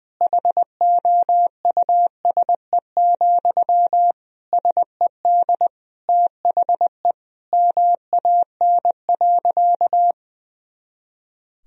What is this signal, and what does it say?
Morse code, heard as a keyed tone, about 20 wpm, 700 Hz. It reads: HOUSE, SED THE MAN.
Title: Sentences from Top 400 Words and Abbreviations 20wpm